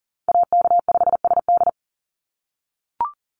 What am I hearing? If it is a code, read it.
Morse code, heard as a keyed tone, about 40 wpm, 700 Hz. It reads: AK5SD
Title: NA SOTA Chaser Callsigns 40wpm